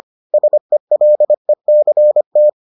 Morse code, heard as a keyed tone, about 25 wpm, 600 Hz. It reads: SELECT